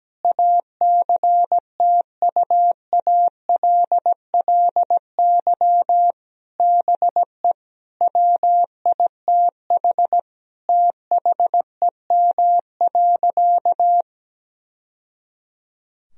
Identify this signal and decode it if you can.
Morse code, heard as a keyed tone, about 17 wpm, 700 Hz. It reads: ACTUALLY BE WITH THEM.
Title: Sentences from Top 1000 Words - Review Part 4 - 17wpm